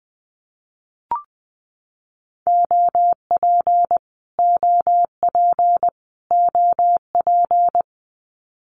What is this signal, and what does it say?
Morse code, heard as a keyed tone, about 19 wpm, 700 Hz. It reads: OP OP OP
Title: Abbreviations Repeated 3x - 20wpm